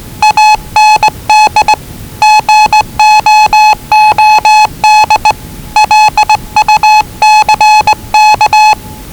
Morse code, heard as a keyed tone, about 20 wpm, 840 Hz. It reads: AND GOOD LUCK
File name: WellingtonRadioZlwCloseDownOnHfAndMf / 01-Zlw 4277khz 417.5 qrt until 0925z